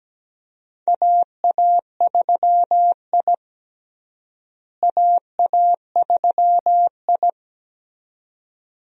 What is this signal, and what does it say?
Morse code, heard as a keyed tone, about 17 wpm, 700 Hz. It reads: AA3I AA3I
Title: USA Call Signs - 4 Characters - Repeated 2x - 17wpm